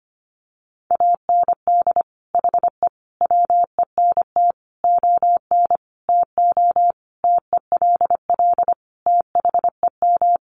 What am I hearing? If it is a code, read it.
Morse code, heard as a keyed tone, about 25 wpm, 700 Hz. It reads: AND HE WENT ON TO TELL THEM
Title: Sentences from Top 900 Words 25wpm